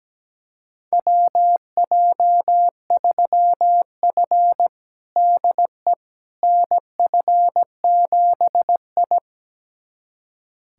Morse code, heard as a keyed tone, about 17 wpm, 700 Hz. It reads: WJ3F DE NF7I
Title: USA Callsign DE Callsign - 4 Characters - 17wpm